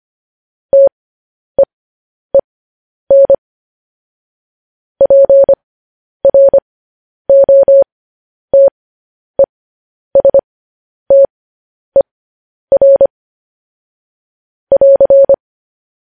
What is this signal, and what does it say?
Morse code, heard as a keyed tone, about 25 wpm, spaced as for 10 wpm, 550 Hz. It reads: TEEN PROTESTER <AR>